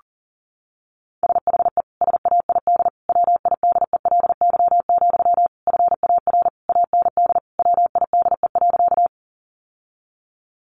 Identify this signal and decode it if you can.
Morse code, heard as a keyed tone, about 40 wpm, 700 Hz. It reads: SHE SAID WIDELY, FAR AND WIDE.
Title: Sentences from Top 1200 Words 40wpm